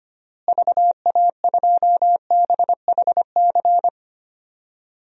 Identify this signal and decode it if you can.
Morse code, heard as a keyed tone, about 25 wpm, 700 Hz. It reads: VA2BHC
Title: DXCC-100 Call Signs - 6 Characters - Repeated 2x - 25wpm